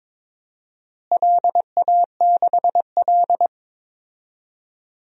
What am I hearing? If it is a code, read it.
Morse code, heard as a keyed tone, about 22 wpm, 700 Hz. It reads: LA6L
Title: DXCC-100 Call Signs - 4 Characters - Repeated 3x - 22wpm